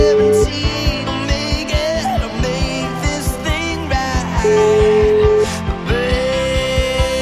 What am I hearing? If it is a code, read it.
Ringback tone (Europe, 425 Hz), with noise.